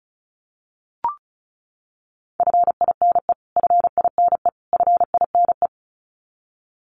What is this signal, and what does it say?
Morse code, heard as a keyed tone, about 34 wpm, 700 Hz. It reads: FINE FINE FINE
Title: Top 500 Words Repeated 35wpm